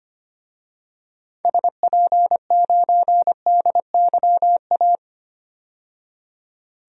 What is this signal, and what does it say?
Morse code, heard as a keyed tone, about 25 wpm, 700 Hz. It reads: SP9DYA
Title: DXCC-100 Call Signs - 6 Characters - 25wpm